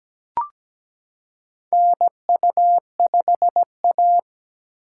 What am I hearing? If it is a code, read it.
Morse code, heard as a keyed tone, about 17 wpm, 700 Hz. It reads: NU5A
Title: K1USN SST QSO Practice - 17wpm